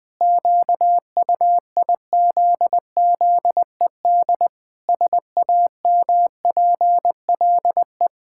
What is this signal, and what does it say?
Morse code, heard as a keyed tone, about 19 wpm, 700 Hz. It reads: QUIZZED SAMPLE
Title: Pangram Challenge v2 - 20wpm